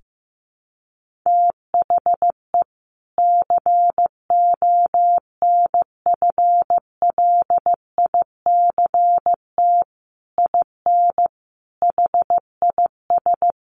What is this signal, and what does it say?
Morse code, heard as a keyed tone, about 15 wpm, 700 Hz. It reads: THE CONFLICT IN HIS
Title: Sentences from Top 1300 Words 15wpm